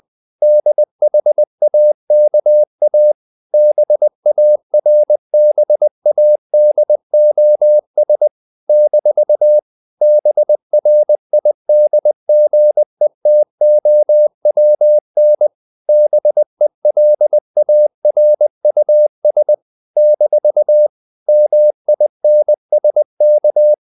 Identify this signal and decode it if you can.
Morse code, heard as a keyed tone, about 20 wpm, 600 Hz. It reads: DHAKA BARBADOS - BRIDGETOWN BELARUS - MINSK
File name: CW / Practice / NationalCapitals20